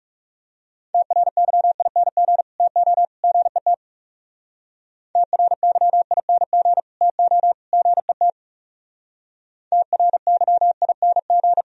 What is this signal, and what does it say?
Morse code, heard as a keyed tone, about 38 wpm, 700 Hz. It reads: TRYING TO GET TRYING TO GET TRYING